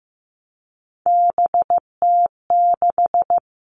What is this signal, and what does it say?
Morse code, heard as a keyed tone, about 15 wpm, 700 Hz. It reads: BT6
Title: All Cut Numbers - 3-Digit 15wpm